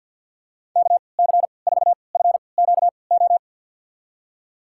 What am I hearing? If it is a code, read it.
Morse code, heard as a keyed tone, about 50 wpm, spaced as for 28 wpm, 700 Hz. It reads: KC4FQO
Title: USA Call Signs - 6 Characters - 3x Character Spacing - 50wpm